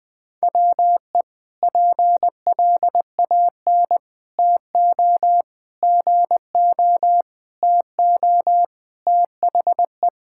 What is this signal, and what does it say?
Morse code, heard as a keyed tone, about 20 wpm, 700 Hz. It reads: WE PLAN TO GO TO THE
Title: Sentences with 4-Word Phrases - Spoken Beginning - 20wpm